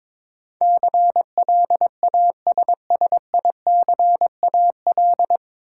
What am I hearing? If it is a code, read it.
Morse code, heard as a keyed tone, about 22 wpm, 700 Hz. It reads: CLASSICAL
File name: 9 Letter Words - First 3 Letters Given - 22wpm